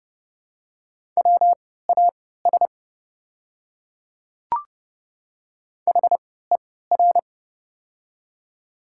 Morse code, heard as a keyed tone, about 30 wpm, spaced as for 17 wpm, 700 Hz. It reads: WAS HER
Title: 3x Character Spacing - Top 100 Words - 30wpm